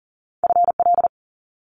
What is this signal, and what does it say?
Morse code, heard as a keyed tone, about 38 wpm, 700 Hz. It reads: FL
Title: Multi-letter Phonemes Quick Fire 40wpm